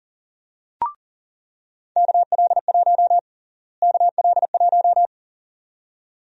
Morse code, heard as a keyed tone, about 40 wpm, 700 Hz. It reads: KL1 KL1
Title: USA Call Sign Prefixes - Repeated 2x - 40wpm